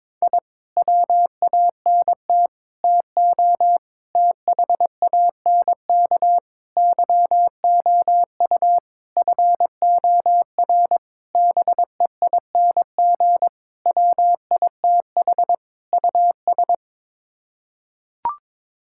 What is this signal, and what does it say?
Morse code, heard as a keyed tone, about 22 wpm, 700 Hz. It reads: I WANT TO THANK YOU FOR BEING WITH US
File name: Sets of 9 Words - 22wpm